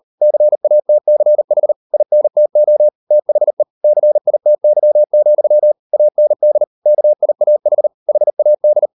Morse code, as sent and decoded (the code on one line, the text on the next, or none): -.-. .- - -.-. .... / .. -. - --- / - .... / -.-. .. - -.-- --..-- / .- -. -.. / -.- .. .- .... / .... .- -..
CATCH INTO THE CITY, AND KIAH HAD